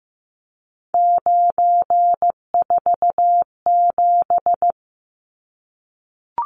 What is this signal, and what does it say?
Morse code, heard as a keyed tone, about 15 wpm, 700 Hz. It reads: 947
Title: Numbers - 3-Digit 15wpm